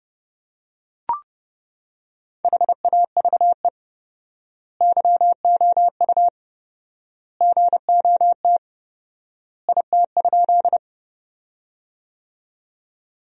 Morse code, heard as a keyed tone, about 30 wpm, 700 Hz. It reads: HAVE YOU GOT IT?
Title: 4x Word-Spacing - Sentences from Top 300 Words - 30wpm